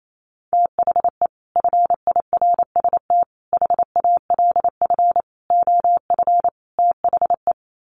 Morse code, heard as a keyed tone, about 28 wpm, 700 Hz. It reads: THE FIRST HALF OF THE
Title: Sets of 5 Words 28wpm